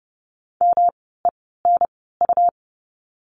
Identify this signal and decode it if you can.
Morse code, heard as a keyed tone, about 30 wpm, spaced as for 17 wpm, 700 Hz. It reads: MENU
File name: course / 30:10wpm / 108 Lesson 18f - Letter m - Words 30wpm